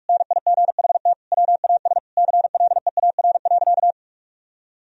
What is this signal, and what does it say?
Morse code, heard as a keyed tone, about 46 wpm, 700 Hz. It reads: NIGHT WAS CLEAR.
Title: Sentences from Top 1000 Words - Review Part 1 - 45wpm